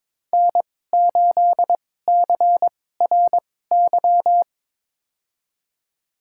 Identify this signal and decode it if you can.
Morse code, heard as a keyed tone, about 22 wpm, spaced as for 16 wpm, 700 Hz. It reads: N8CRY
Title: USA Call Signs - 5 Characters - 2x Character Spacing - 22wpm